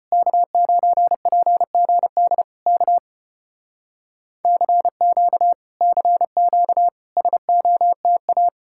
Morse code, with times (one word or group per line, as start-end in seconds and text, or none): 0.1-2.4 K9PGD
2.7-3.0 K
4.4-5.5 CQ
5.8-6.9 CQ
7.2-8.5 SOTA